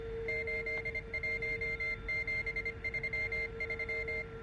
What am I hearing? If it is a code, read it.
Morse code, heard as a keyed tone, about 25 wpm, 2100 Hz, with noise. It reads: 81733